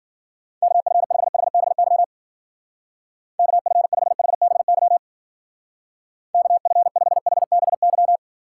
Morse code, heard as a keyed tone, about 50 wpm, 700 Hz. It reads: KF5HBY KF5HBY KF5HBY